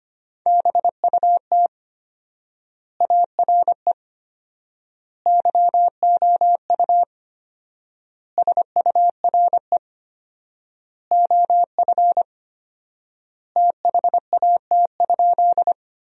Morse code, heard as a keyed tone, about 25 wpm, 700 Hz. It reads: BUT ARE YOU SURE OF THAT?
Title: 4x Word-Spacing - Sentences from Top 300 Words - 25wpm